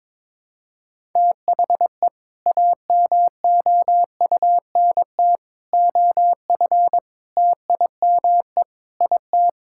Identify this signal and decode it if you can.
Morse code, heard as a keyed tone, about 22 wpm, 700 Hz. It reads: THE AMOUNT OF TIME IT
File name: practice / 22wpm / Sets of 7 Words - 22wpm